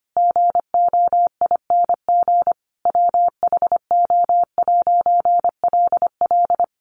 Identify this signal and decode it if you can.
Morse code, heard as a keyed tone, about 25 wpm, 700 Hz. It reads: GOING WHO'LL
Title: A Sound Of Thunder 25wpm